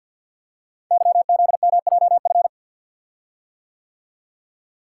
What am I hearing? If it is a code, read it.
Morse code, heard as a keyed tone, about 48 wpm, 700 Hz. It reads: YZMJF
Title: Code Groups - 5 Letters - 50wpm